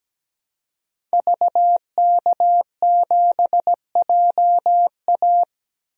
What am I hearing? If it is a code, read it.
Morse code, heard as a keyed tone, about 17 wpm, 700 Hz. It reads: VK7JA
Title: DXCC-100 Call Signs - 5 Characters - 17wpm